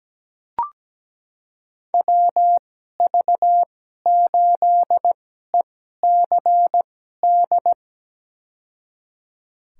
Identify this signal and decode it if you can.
Morse code, heard as a keyed tone, about 17 wpm, spaced as for 12 wpm, 700 Hz. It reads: WV8ECD
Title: USA Call Signs - 6 Characters - 2x Character Spacing - 17wpm